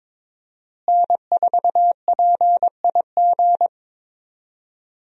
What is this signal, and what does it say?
Morse code, heard as a keyed tone, about 22 wpm, 700 Hz. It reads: N4PIG
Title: USA Call Signs - 5 Characters - Repeated 2x - 22wpm